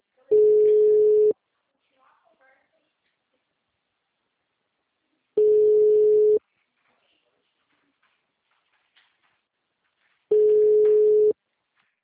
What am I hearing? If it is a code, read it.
Ringback tone (Europe, 425 Hz), with speech.